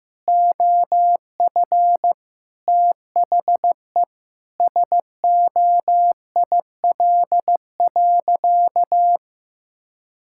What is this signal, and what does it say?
Morse code, heard as a keyed tone, about 15 wpm, 700 Hz. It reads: OF THE SOIL.
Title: Sentences from Top 1600 Words 15wpm